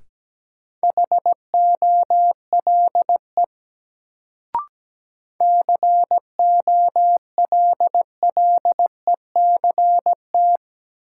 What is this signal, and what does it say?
Morse code, heard as a keyed tone, about 17 wpm, 700 Hz. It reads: HOLE COLLECT
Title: Sentences from Top 1600 Words - New Word Drill - 17wpm